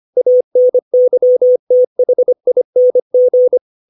Morse code, heard as a keyed tone, about 25 wpm, 500 Hz. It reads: ANYTHING
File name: 632 words / 632 Words 25WPM